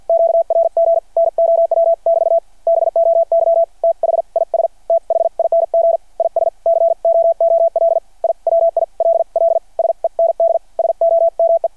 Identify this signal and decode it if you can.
Morse code, heard as a keyed tone, about 53 wpm, spaced as for 45 wpm, 660 Hz. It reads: QRG NOW <BT> BOY THIS THING IS COOL I WILL SEND SOME
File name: archived / CRSnet